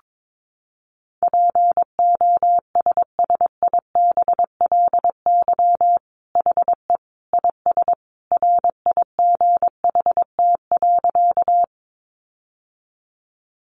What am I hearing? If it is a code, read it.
Morse code, heard as a keyed tone, about 22 wpm, 700 Hz. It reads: POSSIBLY HE IS RIGHT.
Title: Sentences from Top 1300 Words 22wpm